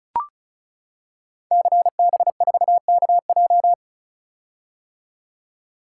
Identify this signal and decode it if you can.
Morse code, heard as a keyed tone, about 36 wpm, 700 Hz. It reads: CB4KJ
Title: Code Groups - 5 Letter:Numbers - 35wpm